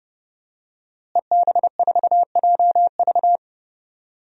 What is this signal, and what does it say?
Morse code, heard as a keyed tone, about 30 wpm, 700 Hz. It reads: EB4JV
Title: DXCC-100 Call Signs - 5 Characters - Repeated 3x - 30wpm